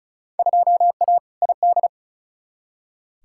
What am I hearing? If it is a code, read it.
Morse code, heard as a keyed tone, about 36 wpm, 700 Hz. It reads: 2A ID
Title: Field Day Exchanges - 35wpm